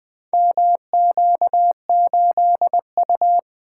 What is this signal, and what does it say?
Morse code, heard as a keyed tone, about 20 wpm, 700 Hz. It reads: MQ8U
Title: Four Letter-Number Combinations - 20wpm